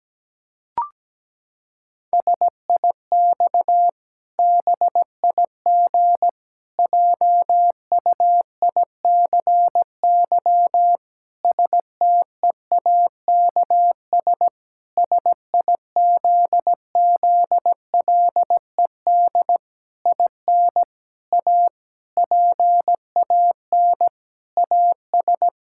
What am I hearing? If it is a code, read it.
Morse code, heard as a keyed tone, about 17 wpm, 700 Hz. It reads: SIX BIG JUICY STEAKS SIZZLED IN A PAN AS
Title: Pangram Challenge v2 - 17wpm